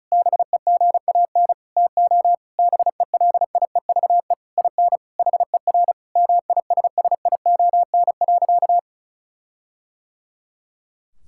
Morse code, heard as a keyed tone, about 36 wpm, 700 Hz. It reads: BEGAN TO BELIEVE IN HER MISSION.
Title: Sentences from Top 1600 Words 35wpm